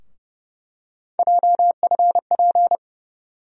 Morse code, heard as a keyed tone, about 30 wpm, 700 Hz. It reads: JFP
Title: USA Call Sign Postfixes - 3 Letters 30wpm